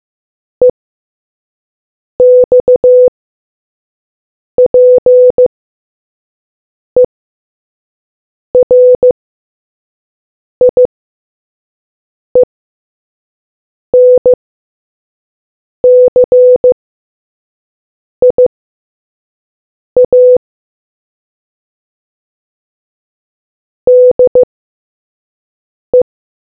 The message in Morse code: EXPERIENCIA DE